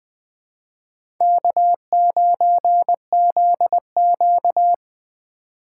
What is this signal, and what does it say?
Morse code, heard as a keyed tone, about 20 wpm, 700 Hz. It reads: K9ZQ
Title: USA Call Signs - 4 Characters - 2x Character Spacing - 20wpm